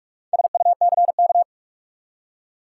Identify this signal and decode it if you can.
Morse code, heard as a keyed tone, about 46 wpm, 700 Hz. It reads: SUCK
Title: Four Letter Words - 45wpm